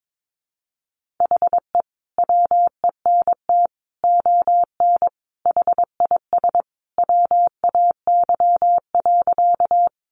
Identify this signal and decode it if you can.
Morse code, heard as a keyed tone, about 22 wpm, 700 Hz. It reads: HE WENT ON HIS WAY.
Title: Sentences from Top 200 Words 22wpm